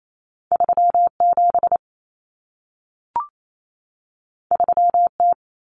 Morse code, heard as a keyed tone, about 28 wpm, 700 Hz. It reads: 37 3T
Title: Common Cut Numbers - 2-Digit 28wpm